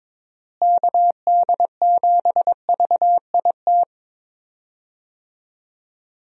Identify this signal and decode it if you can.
Morse code, heard as a keyed tone, about 22 wpm, 700 Hz. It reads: KD7VIT
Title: USA Call Signs - 6 Characters - Repeated 2x - 22wpm